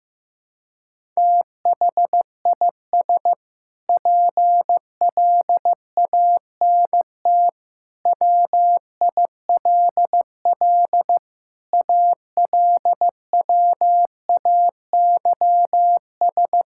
Morse code, heard as a keyed tone, about 16 wpm, 700 Hz. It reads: THIS PLANT WILL ALWAYS
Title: Sentences from Top 1000 Words - Review Part 6 - 15wpm